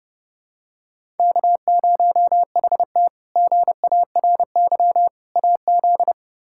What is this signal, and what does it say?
Morse code, heard as a keyed tone, about 30 wpm, 700 Hz. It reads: K0HT GARY AZ
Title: K1USN SST QSO Practice - 30wpm